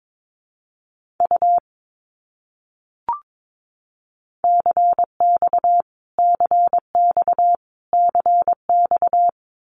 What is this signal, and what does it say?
Morse code, heard as a keyed tone, about 22 wpm, 700 Hz. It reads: U CX CX CX
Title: Abbreviations Repeated 3x - 22wpm